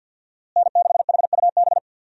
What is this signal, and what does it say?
Morse code, heard as a keyed tone, about 50 wpm, 700 Hz. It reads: N6HUB
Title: USA Call Signs - 5 Characters - Repeated 3x - 50wpm